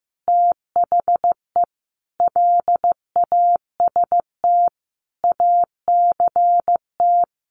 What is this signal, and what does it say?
Morse code, heard as a keyed tone, about 15 wpm, 700 Hz. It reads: THE LAST ACT
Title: Sentences from Top 1500 Words 15wpm